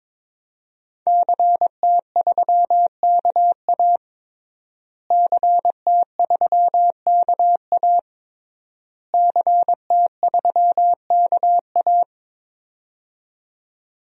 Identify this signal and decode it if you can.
Morse code, heard as a keyed tone, about 22 wpm, 700 Hz. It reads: CT3KA CT3KA CT3KA
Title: DXCC-100 Call Signs - 5 Characters - Repeated 3x - 22wpm